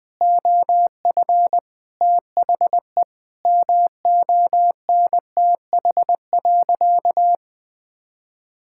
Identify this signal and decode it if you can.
Morse code, heard as a keyed tone, about 21 wpm, 700 Hz. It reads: OF THE MONTH.
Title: Sentences from Top 1000 Words - Review Part 5 - 20wpm